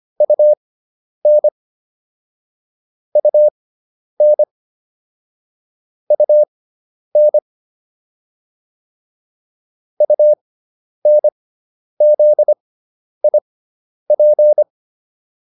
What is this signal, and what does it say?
Morse code, heard as a keyed tone, about 25 wpm, spaced as for 10 wpm, 600 Hz. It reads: UN UN UN UNZIP